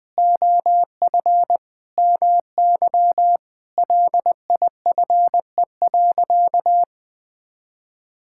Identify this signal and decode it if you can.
Morse code, heard as a keyed tone, about 20 wpm, 700 Hz. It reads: OF MY LIFE.
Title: Sentences with 5-Word Phrases - 20wpm